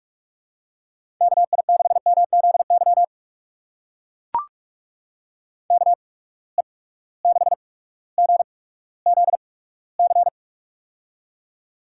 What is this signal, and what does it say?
Morse code, heard as a keyed tone, about 42 wpm, spaced as for 19 wpm, 700 Hz. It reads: KI6MZY KE6GZC
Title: USA Call Signs - 6 Characters - 8x Character Spacing - 45wpm